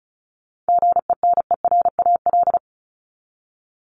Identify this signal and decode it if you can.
Morse code, heard as a keyed tone, about 34 wpm, 700 Hz. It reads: GENERAL
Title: LY Suffix - Top 100 Words with Base Words 35wpm